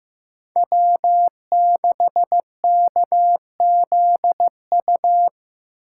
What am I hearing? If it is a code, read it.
Morse code, heard as a keyed tone, about 15 wpm, 700 Hz. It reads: W6KZU
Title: USA Call Signs - 5 Characters - 2x Character Spacing - 15wpm